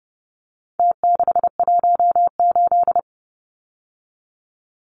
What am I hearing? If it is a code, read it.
Morse code, heard as a keyed tone, about 30 wpm, 700 Hz. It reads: T618
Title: Numbers - Contest Style - 4-Digit 30wpm